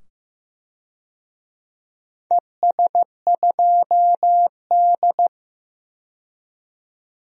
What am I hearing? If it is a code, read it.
Morse code, heard as a keyed tone, about 15 wpm, 700 Hz. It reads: ES2D
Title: DXCC-100 Call Signs - 4 Characters - 15wpm